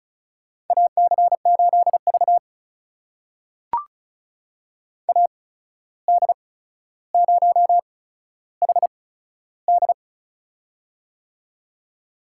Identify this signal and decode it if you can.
Morse code, heard as a keyed tone, about 34 wpm, spaced as for 14 wpm, 700 Hz. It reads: AC8V AD0HD
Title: USA Call Signs - 8x Character Spacing - 35wpm